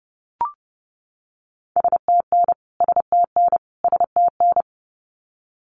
Morse code, heard as a keyed tone, about 30 wpm, 700 Hz. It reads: STN STN STN
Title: Abbreviations Repeated 3x - 30wpm